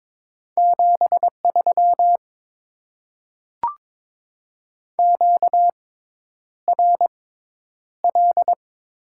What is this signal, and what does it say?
Morse code, heard as a keyed tone, about 22 wpm, spaced as for 10 wpm, 700 Hz. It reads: 73 QRL
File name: Top 20 QSO Elements - 6x Character Spacing - 22wpm